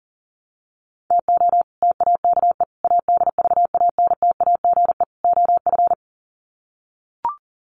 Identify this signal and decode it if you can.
Morse code, heard as a keyed tone, about 38 wpm, 700 Hz. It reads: TO TAKE ADVANTAGE OF
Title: Sets of 4 Words 40wpm